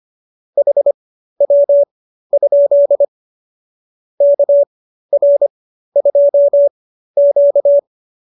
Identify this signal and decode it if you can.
Morse code, heard as a keyed tone, about 25 wpm, spaced as for 13 wpm, 570 Hz. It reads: HW? KR2Q